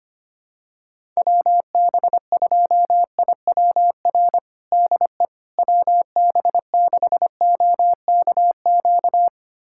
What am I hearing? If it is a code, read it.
Morse code, heard as a keyed tone, about 25 wpm, 700 Hz. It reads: WB2IWR DE WB6OKQ
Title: USA Callsign DE Callsign 25wpm